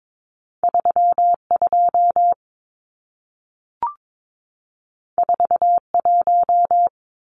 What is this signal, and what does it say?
Morse code, heard as a keyed tone, about 22 wpm, 700 Hz. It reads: 32 41
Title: Numbers - 2-Digit 22wpm